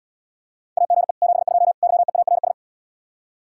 Morse code, heard as a keyed tone, about 80 wpm, 700 Hz. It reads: AGE 72 YEARS